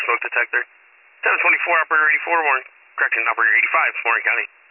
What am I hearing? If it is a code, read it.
Emergency Alert System (SAME) tones, heard as harsh buzzing bursts, with speech. It is heard from 0.3 to 3.7 s.